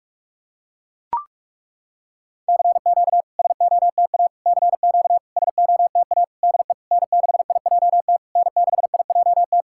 Morse code, heard as a keyed tone, about 43 wpm, 690 Hz. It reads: CQ SOTA CQ SOTA DE N6IJT N6IJT
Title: SOTA QSO Practice 45wpm